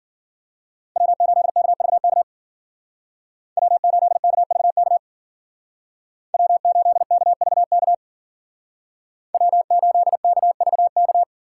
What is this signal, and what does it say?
Morse code, heard as a keyed tone, about 52 wpm, 700 Hz. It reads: W8KVK W8KVK W8KVK W8KVK